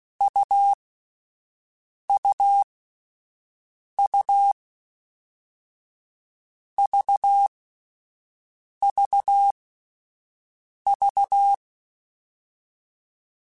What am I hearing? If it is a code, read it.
Morse code, heard as a keyed tone, about 16 wpm, 800 Hz. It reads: U U U V V V
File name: Difficult Combos / Difficult Combos-mp3s / AUV AUV4 @10